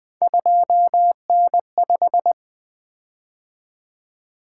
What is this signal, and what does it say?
Morse code, heard as a keyed tone, about 20 wpm, 700 Hz. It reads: 2N5
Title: Three Letter-Number Combinations - 20wpm